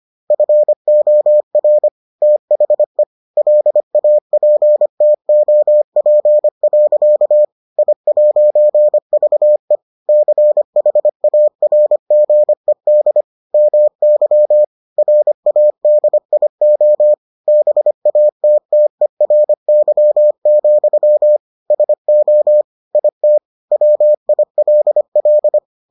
Morse code, heard as a keyed tone, about 25 wpm, 600 Hz. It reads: FOR THE LAPTOP. I'VE CHARGED MY RADIO BATTERY, SO IT WILL